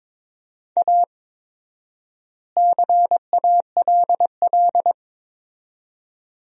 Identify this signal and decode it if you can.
Morse code, heard as a keyed tone, about 22 wpm, 700 Hz. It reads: A CALL
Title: 4x Word-Spacing - Sentences from Top 300 Words - 22wpm